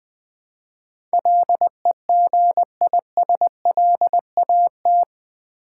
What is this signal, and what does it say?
Morse code, heard as a keyed tone, about 20 wpm, 700 Hz. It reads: LEGISLAT
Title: Build A Word - 4 Letters At A Time - 11-Letter Words - 20wpm